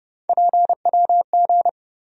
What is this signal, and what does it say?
Morse code, heard as a keyed tone, about 30 wpm, 700 Hz. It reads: PWG
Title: USA Call Sign Postfixes - 3 Letters 30wpm